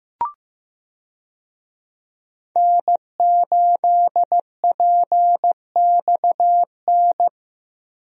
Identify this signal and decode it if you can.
Morse code, heard as a keyed tone, about 15 wpm, 700 Hz. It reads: N8PXN
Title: USA Call Signs - 5 Characters - Repeated 3x - 15wpm